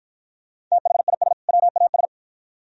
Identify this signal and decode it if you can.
Morse code, heard as a keyed tone, about 48 wpm, 700 Hz. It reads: THIS WAS